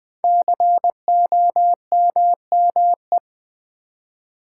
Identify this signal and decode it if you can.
Morse code, heard as a keyed tone, about 20 wpm, 700 Hz. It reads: COMME
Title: Build A Word - 5 Letters At A Time - 10-Letter Words - 20wpm